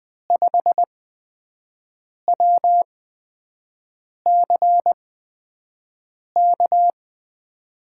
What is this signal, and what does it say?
Morse code, heard as a keyed tone, about 20 wpm, 700 Hz. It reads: 5 W C K
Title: USA Call Signs - 8x Character Spacing - 20wpm